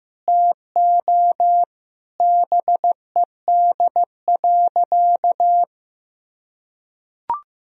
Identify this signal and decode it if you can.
Morse code, heard as a keyed tone, about 15 wpm, 700 Hz. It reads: TO BED.
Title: Sentences from Top 1000 Words - Review Part 2 - 15wpm